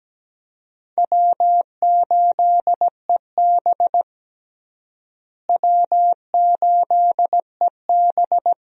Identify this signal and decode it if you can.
Morse code, heard as a keyed tone, about 17 wpm, 700 Hz. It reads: W8EB W8EB
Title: USA Call Signs - 4 Characters - Repeated 2x - 17wpm